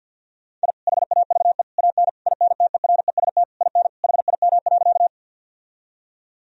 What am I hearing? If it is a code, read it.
Morse code, heard as a keyed tone, about 50 wpm, 700 Hz. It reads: I HAVE AN INTEREST IN HIM.
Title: Sentences from Top 1000 Words - Review Part 2 - 50wpm